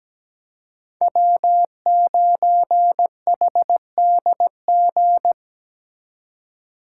Morse code, heard as a keyed tone, about 20 wpm, 700 Hz. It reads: W9HDG
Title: Speed Racing - USA 1x3 Call Signs 17wpm